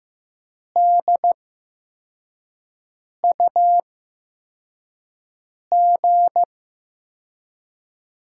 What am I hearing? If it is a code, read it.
Morse code, heard as a keyed tone, about 15 wpm, spaced as for 4 wpm, 700 Hz. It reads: DUG
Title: USA Call Signs - 5 Characters - 8x Character Spacing - 15wpm